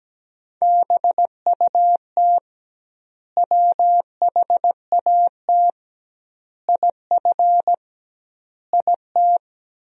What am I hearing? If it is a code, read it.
Morse code, heard as a keyed tone, about 17 wpm, 700 Hz. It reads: BUT WHAT IF IT
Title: 2x Word-Spacing - Sentences from Top 100 Words - 17wpm